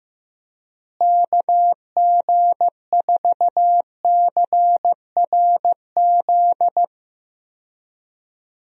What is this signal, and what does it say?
Morse code, heard as a keyed tone, about 15 wpm, 700 Hz. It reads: KG4CRZ
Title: USA Call Signs - 6 Characters - Repeated 2x - 15wpm